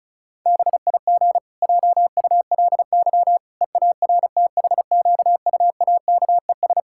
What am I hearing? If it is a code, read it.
Morse code, heard as a keyed tone, about 36 wpm, 700 Hz. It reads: BIG JULY EARTHQUAKES
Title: Pangram Challenge v2 - 35wpm